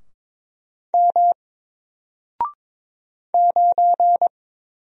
Morse code, heard as a keyed tone, about 23 wpm, 700 Hz. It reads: M 9
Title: Single Character (Letter:Number:Punctuation:Prosign) Alt 22wpm